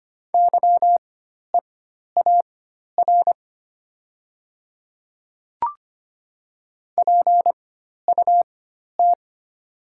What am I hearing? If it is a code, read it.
Morse code, heard as a keyed tone, about 25 wpm, spaced as for 12 wpm, 700 Hz. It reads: YEAR PUT
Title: 4x Character Spacing - Top 300 Words - 25wpm